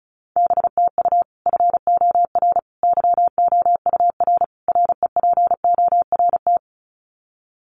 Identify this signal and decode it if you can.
Morse code, heard as a keyed tone, about 34 wpm, 700 Hz. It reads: BTU FOR YOUR REPORT
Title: Common QSO Phrases 35wpm